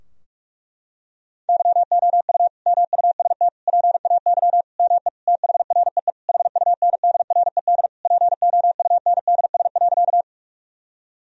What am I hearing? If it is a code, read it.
Morse code, heard as a keyed tone, about 46 wpm, 700 Hz. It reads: YOU MUST PAY ME THREE HUNDRED POUNDS.
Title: Sentences from Top 1000 Words - Review Part 2 - 45wpm